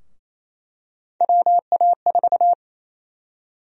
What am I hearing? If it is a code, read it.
Morse code, heard as a keyed tone, about 28 wpm, 700 Hz. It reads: WA4
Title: USA Call Sign Prefixes - 28wpm